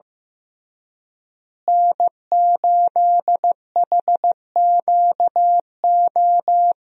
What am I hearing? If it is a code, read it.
Morse code, heard as a keyed tone, about 15 wpm, 700 Hz. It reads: N8HQO
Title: USA Call Signs - 5 Characters - Repeated 3x - 15wpm